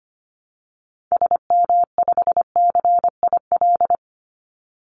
Morse code, heard as a keyed tone, about 25 wpm, 700 Hz. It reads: SM5CIL
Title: Call Signs - 25wpm